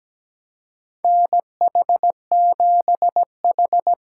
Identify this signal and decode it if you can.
Morse code, heard as a keyed tone, about 17 wpm, 700 Hz. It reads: NH7H
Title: USA Call Signs - 2x Character Spacing - 17wpm